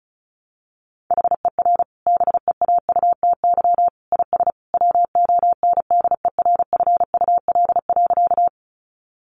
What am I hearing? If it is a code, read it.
Morse code, heard as a keyed tone, about 34 wpm, 700 Hz. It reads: HER BEAUTY IS WONDERFUL.
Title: Sentences from Top 1100 Words 35wpm